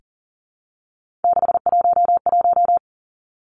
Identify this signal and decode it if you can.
Morse code, heard as a keyed tone, about 42 wpm, 700 Hz. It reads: 611
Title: Three Challenging Letter-Number Combinations 40wpm